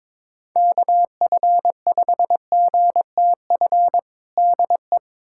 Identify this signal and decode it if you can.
Morse code, heard as a keyed tone, about 22 wpm, 700 Hz. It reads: KF5GTF DE
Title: USA Callsign DE Callsign 22wpm